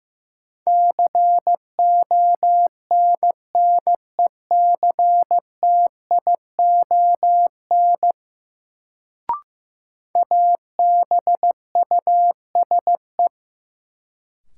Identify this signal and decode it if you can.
Morse code, heard as a keyed tone, about 15 wpm, 700 Hz. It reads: CONNECTION ABUSE
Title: Sentences from Top 1300 Words - New Word Drill - 15wpm